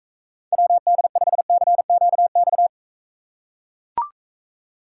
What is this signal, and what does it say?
Morse code, heard as a keyed tone, about 42 wpm, 700 Hz. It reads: WD5CQX